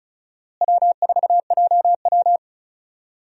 Morse code, heard as a keyed tone, about 36 wpm, 700 Hz. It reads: W4JW
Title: USA Call Signs - 4 Characters - Repeated 3x - 35wpm